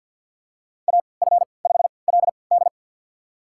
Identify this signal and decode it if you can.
Morse code, heard as a keyed tone, about 48 wpm, spaced as for 28 wpm, 700 Hz. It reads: AF5LD
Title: USA Call Signs - 5 Characters - 3x Character Spacing - 50wpm